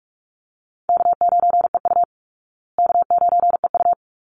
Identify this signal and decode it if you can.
Morse code, heard as a keyed tone, about 44 wpm, 700 Hz. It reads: K9EU K9EU
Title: Call Signs Repeated 3x - 45wpm